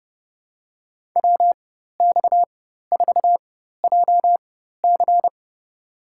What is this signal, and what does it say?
Morse code, heard as a keyed tone, about 32 wpm, spaced as for 14 wpm, 700 Hz. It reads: WX4JC
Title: USA Call Signs - 5 Characters - 4x Character Spacing - 30wpm